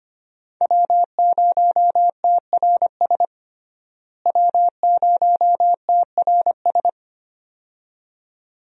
Morse code, heard as a keyed tone, about 25 wpm, 700 Hz. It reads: W0TRS W0TRS
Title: USA Call Signs - 5 Characters - Repeated 2x - 25wpm